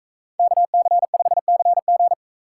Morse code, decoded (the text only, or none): KC5CG